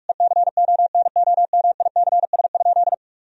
Morse code, heard as a keyed tone, about 48 wpm, 700 Hz. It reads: ECONOMICS?
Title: Sentences from Top 1000 Words - Review Part 3 - 45wpm